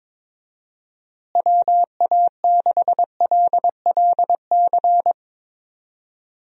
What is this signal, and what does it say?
Morse code, heard as a keyed tone, about 22 wpm, 700 Hz. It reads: WA6LLC
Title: USA Call Signs - 6 Characters - Repeated 3x - 22wpm